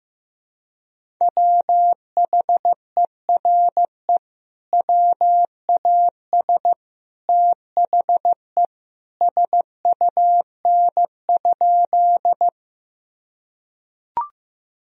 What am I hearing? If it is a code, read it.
Morse code, heard as a keyed tone, about 15 wpm, 700 Hz. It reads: WHERE WAS THE SUN?
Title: Sentences from Top 1000 Words - Review Part 1 - 15wpm